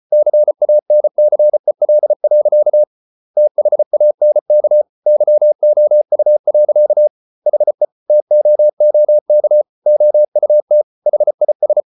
Morse code, heard as a keyed tone, about 34 wpm, 600 Hz. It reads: CANCEL. THANK YOU. HE TOOK OUT HIS